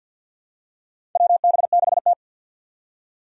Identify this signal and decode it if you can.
Morse code, heard as a keyed tone, about 52 wpm, 700 Hz. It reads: WB6T
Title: Four Challenging Letter-Number Combinations 50wpm